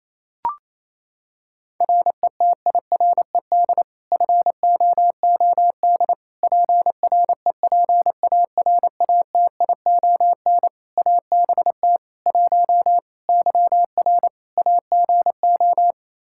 Morse code, heard as a keyed tone, about 27 wpm, 700 Hz. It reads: RETIRED FOOD PREPARATION ABT 1 YR AGO
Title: Practice QSO - Exchange 3 - More Info - 28wpm